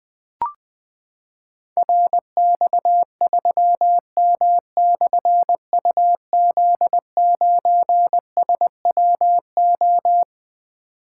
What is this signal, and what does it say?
Morse code, heard as a keyed tone, about 20 wpm, 700 Hz. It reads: RX3M/UZ9SWO
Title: Difficult Call Signs 20wpm